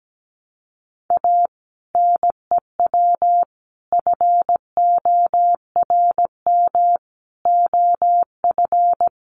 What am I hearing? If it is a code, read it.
Morse code, heard as a keyed tone, about 17 wpm, 700 Hz. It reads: A NEW FORM OF
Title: Sentences from Top 1300 Words 17wpm